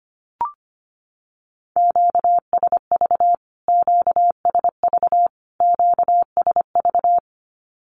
Morse code, heard as a keyed tone, about 25 wpm, 700 Hz. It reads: QSV QSV QSV
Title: Q-Codes Repeated 3x 25wpm